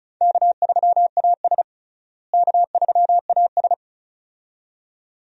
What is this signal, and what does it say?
Morse code, heard as a keyed tone, about 34 wpm, 700 Hz. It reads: K3AS K3AS
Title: USA Call Signs - 4 Characters - Repeated 2x - 35wpm